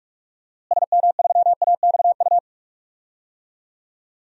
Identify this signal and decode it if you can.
Morse code, heard as a keyed tone, about 46 wpm, 700 Hz. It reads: SM3AXU
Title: DXCC-100 Call Signs - 6 Characters - Repeated 3x - 45wpm